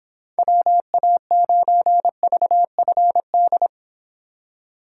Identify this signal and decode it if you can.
Morse code, heard as a keyed tone, about 26 wpm, 700 Hz. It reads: WA9VFD